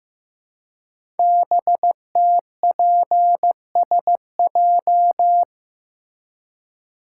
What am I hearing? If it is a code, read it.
Morse code, heard as a keyed tone, about 15 wpm, 700 Hz. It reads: BTPSJ
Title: Code Groups - 5 Letters - 15wpm